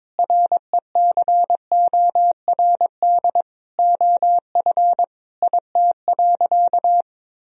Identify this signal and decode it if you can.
Morse code, heard as a keyed tone, about 23 wpm, 700 Hz. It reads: RECORD OF IT.
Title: Sentences from Top 1000 Words - Review Part 3 - 22wpm